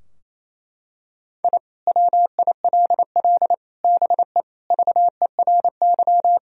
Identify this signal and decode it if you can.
Morse code, heard as a keyed tone, about 28 wpm, 700 Hz. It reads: I WILL BE VERY
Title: Sentences from Top 1000 Words - Review Part 3 - 28wpm